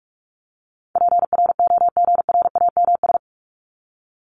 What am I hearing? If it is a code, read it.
Morse code, heard as a keyed tone, about 44 wpm, 700 Hz. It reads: PROGRAMS
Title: ES:S Suffix - Top 100 Words with Base Words 45wpm